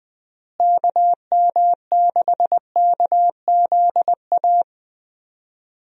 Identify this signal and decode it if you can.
Morse code, heard as a keyed tone, about 20 wpm, 700 Hz. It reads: KM6KZA
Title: USA Call Signs - 6 Characters - Repeated 2x - 20wpm